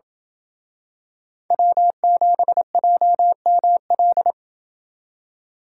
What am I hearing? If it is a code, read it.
Morse code, heard as a keyed tone, about 27 wpm, 700 Hz. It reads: W7JML